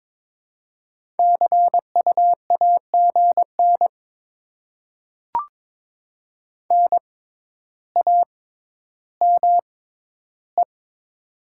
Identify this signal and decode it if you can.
Morse code, heard as a keyed tone, about 22 wpm, spaced as for 11 wpm, 700 Hz. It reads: CU AGN NAME
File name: Top 20 QSO Elements - 6x Character Spacing - 22wpm